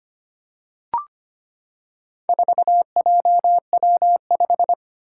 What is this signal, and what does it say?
Morse code, heard as a keyed tone, about 25 wpm, 700 Hz. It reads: 4JW5
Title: Four Challenging Letter-Number Combinations 25wpm